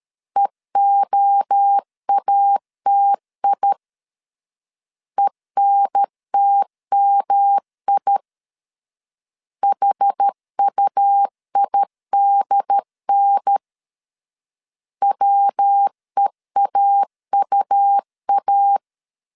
Morse code, heard as a keyed tone, about 13 wpm, 800 Hz. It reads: EOATI ENTMI HUIDN WEAUA
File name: CW / DET / 1 3 1500C S C 12wpm 2750